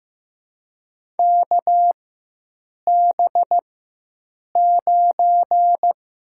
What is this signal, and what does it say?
Morse code, heard as a keyed tone, about 15 wpm, 700 Hz. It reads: K B 9
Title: USA Call Signs - 6 Characters - 4x Character Spacing - 15wpm